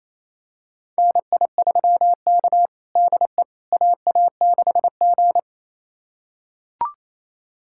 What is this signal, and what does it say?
Morse code, heard as a keyed tone, about 28 wpm, 700 Hz. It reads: NI3K DE AA6G
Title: USA Callsign DE Callsign - 4 Characters - 28wpm